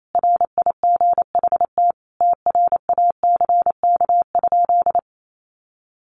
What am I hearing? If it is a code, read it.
Morse code, heard as a keyed tone, about 28 wpm, 700 Hz. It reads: RIGHT TRACK?
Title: Sentences from Top 1100 Words 28wpm